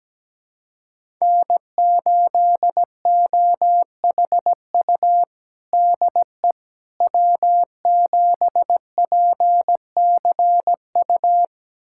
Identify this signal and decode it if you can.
Morse code, heard as a keyed tone, about 17 wpm, 700 Hz. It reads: N8OHU DE W7PCU
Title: USA Callsign DE Callsign - 5 Characters - 17wpm